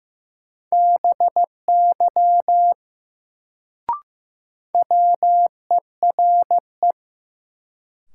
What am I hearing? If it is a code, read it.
Morse code, heard as a keyed tone, about 15 wpm, 700 Hz. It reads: BY WERE
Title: Sentences from Top 100 Words - New Word Drill - 15wpm